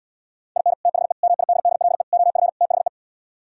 Morse code, heard as a keyed tone, about 77 wpm, 700 Hz. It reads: IN THE MIDDLE OF THE